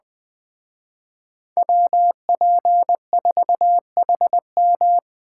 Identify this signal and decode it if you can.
Morse code, heard as a keyed tone, about 20 wpm, 700 Hz. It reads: WP4HM